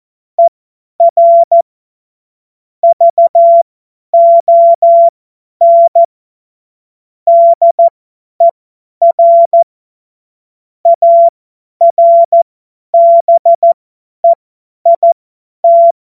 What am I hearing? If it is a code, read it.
Morse code, heard as a keyed tone, about 14 wpm, spaced as for 10 wpm, 680 Hz. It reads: ER VON DER ARBEIT
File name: Morsekurs / Witze / Witz 02 April 2024 50 F70